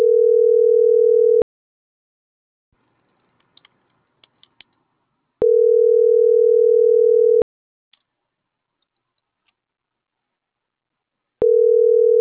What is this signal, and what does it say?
Ringback tone (North America), 440 + 480 Hz.